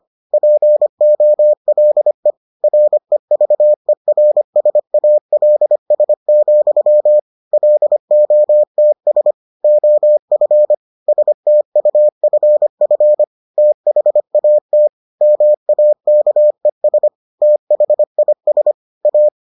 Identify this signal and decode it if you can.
Morse code, heard as a keyed tone, about 25 wpm, 600 Hz. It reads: POLE REVERSALS, LOTS OF STUFF THAT MAKES THIS A